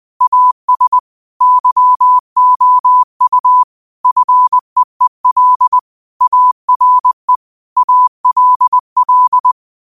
Morse code, heard as a keyed tone, about 20 wpm, 1000 Hz. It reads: AS YOU FEEL ARE ALL